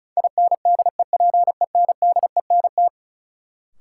Morse code, heard as a keyed tone, about 34 wpm, 700 Hz. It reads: INDEPENDENT
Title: Sentences from Top 900 Words - New Word Drill - 35wpm